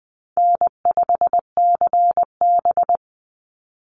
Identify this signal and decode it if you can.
Morse code, heard as a keyed tone, about 20 wpm, 700 Hz. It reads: N5CB
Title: USA Call Signs 20wpm